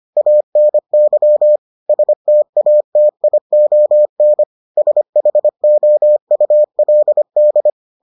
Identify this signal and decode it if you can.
Morse code, heard as a keyed tone, about 25 wpm, 600 Hz. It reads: ANY STATION SHOULD